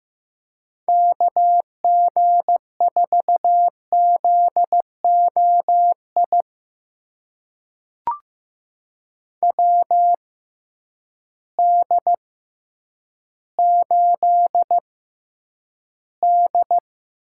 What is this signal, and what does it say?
Morse code, heard as a keyed tone, about 15 wpm, spaced as for 9 wpm, 700 Hz. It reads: KG4ZOI WD8D
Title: USA Call Signs - 6 Characters - 6x Character Spacing - 15wpm